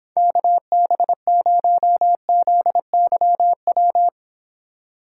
Morse code, heard as a keyed tone, about 29 wpm, 700 Hz. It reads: KB0ZYW